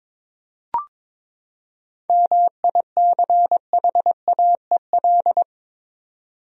Morse code, heard as a keyed tone, about 21 wpm, 700 Hz. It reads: MICHAEL
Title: USA Common Names - 22wpm